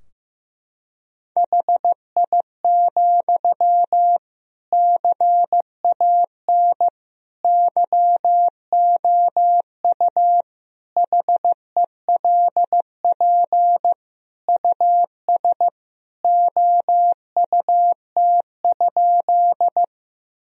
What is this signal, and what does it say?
Morse code, heard as a keyed tone, about 15 wpm, 700 Hz. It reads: HI, CAN YOU HELP US OUT?
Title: Sentences from Top 1000 Words - Review Part 3 - 15wpm